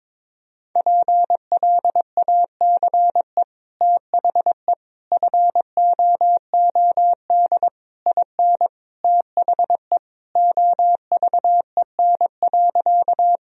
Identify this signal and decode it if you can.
Morse code, heard as a keyed tone, about 22 wpm, 700 Hz. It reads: PLACE THE FOOD IN THE OVEN.
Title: Sentences from Top 1000 Words - Review Part 4 - 22wpm